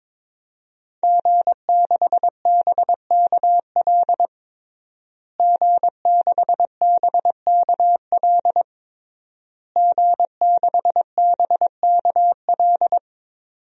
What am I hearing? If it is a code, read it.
Morse code, heard as a keyed tone, about 22 wpm, 700 Hz. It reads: G6BKL G6BKL G6BKL
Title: DXCC-100 Call Signs - 5 Characters - Repeated 3x - 22wpm